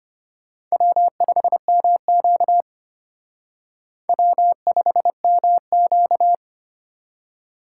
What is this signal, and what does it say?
Morse code, heard as a keyed tone, about 31 wpm, 700 Hz. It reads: W5MQ W5MQ
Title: Speed Racing - USA Call Signs 25wpm